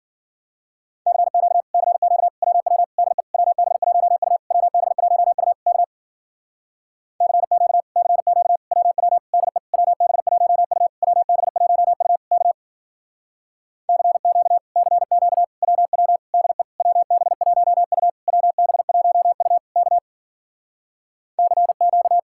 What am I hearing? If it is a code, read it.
Morse code, heard as a keyed tone, about 54 wpm, 700 Hz. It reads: CQ CQ WW DE WB1U WB1U K CQ CQ WW DE WB1U WB1U K CQ CQ WW DE WB1U WB1U K CQ